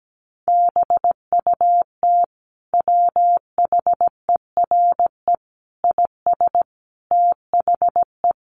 Morse code, heard as a keyed tone, about 17 wpm, 700 Hz. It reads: BUT WHERE IS THE
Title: Sentences from Top 1500 Words 17wpm